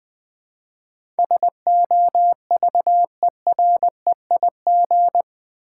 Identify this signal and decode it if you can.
Morse code, heard as a keyed tone, about 20 wpm, 700 Hz. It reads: SOVEREIG
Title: Build A Word - 4 Letters At A Time - 11-Letter Words - 20wpm